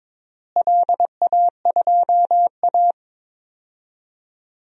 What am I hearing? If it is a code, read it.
Morse code, heard as a keyed tone, about 22 wpm, 700 Hz. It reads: LA2A
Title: DXCC-100 Call Signs - 4 Characters - Repeated 2x - 22wpm